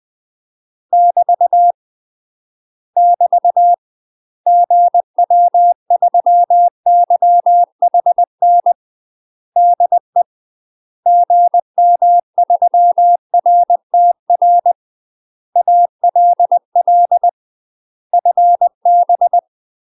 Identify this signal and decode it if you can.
Morse code, heard as a keyed tone, about 20 wpm, 700 Hz. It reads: <BT> <BT> GW3YHN DE GM3RTR ALL FB
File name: UK MASTER QSOs TRAINING / 20WPM / UK MASTER QSOs 0014